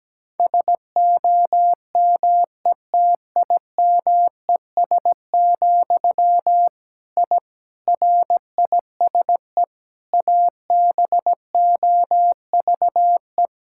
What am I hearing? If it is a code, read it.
Morse code, heard as a keyed tone, about 17 wpm, 700 Hz. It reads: SOMETIMES, I RISE ABOVE
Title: Sentences from Top 1000 Words - Review Part 6 - 17wpm